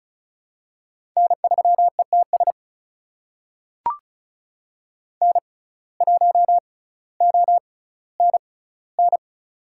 Morse code, heard as a keyed tone, about 36 wpm, spaced as for 18 wpm, 700 Hz. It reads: N3ETS N1ONN
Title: USA Call Signs - 5 Characters - 6x Character Spacing - 35wpm